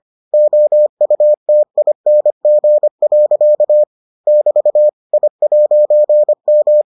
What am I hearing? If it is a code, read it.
Morse code, heard as a keyed tone, about 25 wpm, 600 Hz. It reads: OUTING. <BT> I'M